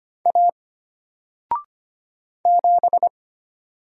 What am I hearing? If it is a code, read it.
Morse code, heard as a keyed tone, about 25 wpm, 700 Hz. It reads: A 7
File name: Single-Letter-Number-25wpm